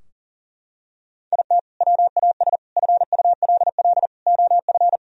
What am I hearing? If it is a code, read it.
Morse code, heard as a keyed tone, about 40 wpm, 700 Hz. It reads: IT WAS FULL OF
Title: Sentences from Top 1000 Words - Review Part 3 - 40wpm